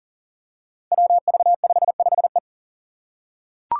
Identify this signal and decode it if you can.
Morse code, heard as a keyed tone, about 42 wpm, 700 Hz. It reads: WV55E
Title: Five Challenging Letter-Number Combinations 40wpm